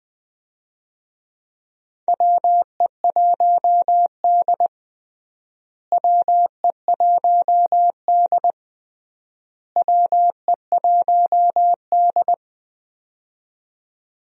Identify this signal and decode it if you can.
Morse code, heard as a keyed tone, about 20 wpm, 700 Hz. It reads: WE1D WE1D WE1D
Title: USA Call Signs - 4 Characters - Repeated 3x - 20wpm